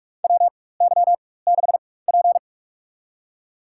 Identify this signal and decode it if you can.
Morse code, heard as a keyed tone, about 43 wpm, spaced as for 21 wpm, 700 Hz. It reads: WY6P
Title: USA Call Signs - 4 Characters - 4x Character Spacing - 45wpm